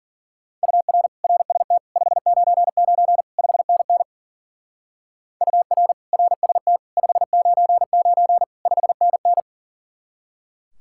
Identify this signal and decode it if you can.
Morse code, heard as a keyed tone, about 49 wpm, 700 Hz. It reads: UR RST 599 5NN UR RST 599 5NN
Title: Speed-Racing Common QSO Phrases - 40wpm